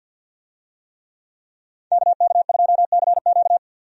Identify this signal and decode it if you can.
Morse code, heard as a keyed tone, about 47 wpm, 700 Hz. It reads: KK2CQ
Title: USA Call Signs - 5 Characters - 50wpm